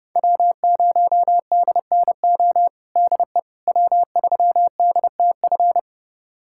Morse code, heard as a keyed tone, about 29 wpm, 700 Hz. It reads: W0DNO DE W3DTF
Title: USA Callsign DE Callsign - 5 Characters - 30wpm